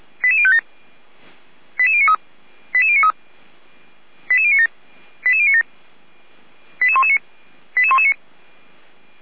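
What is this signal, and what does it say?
Five-tone selective call (ZVEI), heard as quick runs of five tones, tones of about 70 ms. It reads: 88057 88093 88093 88097 88097 88119 88119
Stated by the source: ZVEI five-tone selective call received on an SDR.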